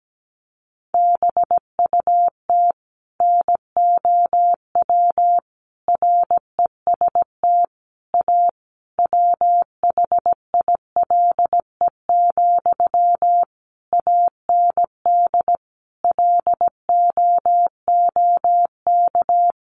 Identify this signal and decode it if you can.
Morse code, heard as a keyed tone, about 17 wpm, 700 Hz. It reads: BUT NOW REST A WHILE, AND LOOK
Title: Sentences from Top 1100 Words 17wpm